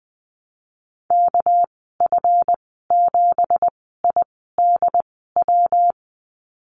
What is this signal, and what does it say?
Morse code, heard as a keyed tone, about 20 wpm, spaced as for 14 wpm, 700 Hz. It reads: KF7IDW
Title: 245 Lesson 39g - Letter z - Call Signs 20wpm